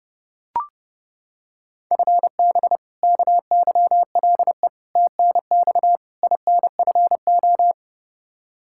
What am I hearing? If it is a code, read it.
Morse code, heard as a keyed tone, about 32 wpm, 700 Hz. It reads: FB KYLE TNX INFO
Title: Practice QSO - Exchange 3 - More Info - 30wpm